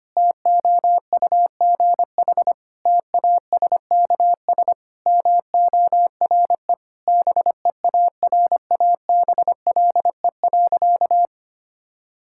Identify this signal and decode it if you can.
Morse code, heard as a keyed tone, about 25 wpm, 700 Hz. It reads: TOUGH TASKS MORE BEARABLE.
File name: practice / 25wpm / Difficult Sentences with Hints - 25wpm